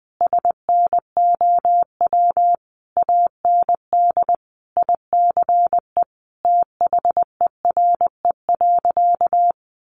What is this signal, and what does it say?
Morse code, heard as a keyed tone, about 20 wpm, 700 Hz. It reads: SNOW AND ICE THERE.
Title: Sentences from Top 1000 Words 20wpm